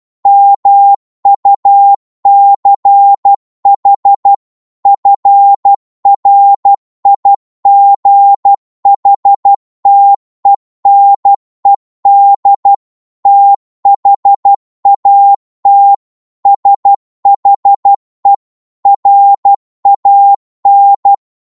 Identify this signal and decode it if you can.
Morse code, heard as a keyed tone, about 12 wpm, 800 Hz. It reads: MUCH FRIGHTENED THAT SHE RAN